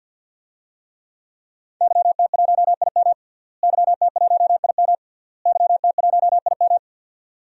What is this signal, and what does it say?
Morse code, heard as a keyed tone, about 48 wpm, 700 Hz. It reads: YT1IM YT1IM YT1IM
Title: DXCC-100 Call Signs - 5 Characters - Repeated 3x - 50wpm